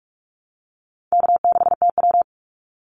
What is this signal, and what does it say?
Morse code, heard as a keyed tone, about 44 wpm, 700 Hz. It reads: K6TW
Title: NA SOTA Activator Callsigns 45wpm